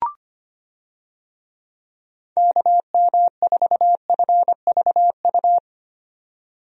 Morse code, heard as a keyed tone, about 25 wpm, 700 Hz. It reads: KM4FVU
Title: USA Call Signs - 6 Characters - 25wpm